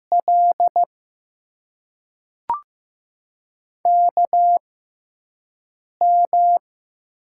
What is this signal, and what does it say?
Morse code, heard as a keyed tone, about 15 wpm, spaced as for 5 wpm, 700 Hz. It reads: L KM
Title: USA Call Signs - 6 Characters - 6x Character Spacing - 15wpm